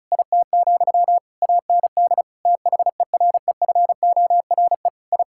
Morse code, heard as a keyed tone, about 34 wpm, 700 Hz. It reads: IT, AND THEREFORE I
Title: Sentences from Top 1000 Words - Review Part 5 - 35wpm